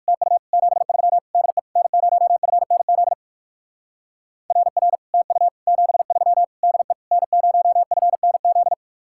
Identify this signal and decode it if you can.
Morse code, heard as a keyed tone, about 53 wpm, 700 Hz. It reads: TU 73 DE N0FNZ RR TU 73 DE N0FNZ
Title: Speed-Racing Common QSO Phrases Encore - 45wpm